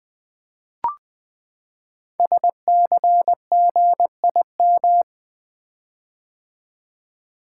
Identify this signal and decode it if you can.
Morse code, heard as a keyed tone, about 20 wpm, 700 Hz. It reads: SCGIM
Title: Code Groups - 5 Letter:Numbers - 20wpm